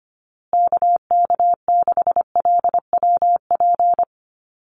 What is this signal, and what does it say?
Morse code, heard as a keyed tone, about 25 wpm, 700 Hz. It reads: KK6LWP
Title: USA Call Signs Repeated 2x 25wpm